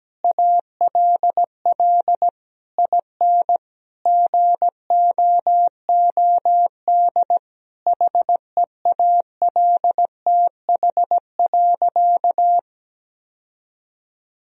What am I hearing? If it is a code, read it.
Morse code, heard as a keyed tone, about 17 wpm, 700 Hz. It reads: ALL IN GOOD HEALTH.
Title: Sentences from Top 1000 Words - Review Part 3 - 17wpm